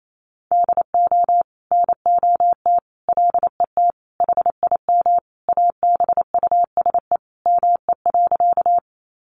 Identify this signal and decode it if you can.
Morse code, heard as a keyed tone, about 28 wpm, 700 Hz. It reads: DO NOT LET HIM ABUSE ME.
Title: Sentences from Top 1300 Words 28wpm